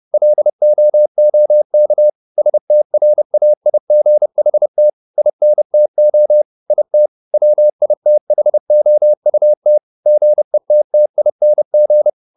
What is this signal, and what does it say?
Morse code, heard as a keyed tone, about 30 wpm, 600 Hz. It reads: LOOK STRAIGHT INTO IT WITHOUT GETTING
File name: the old man and the sea / the old man and the sea 1 30wpm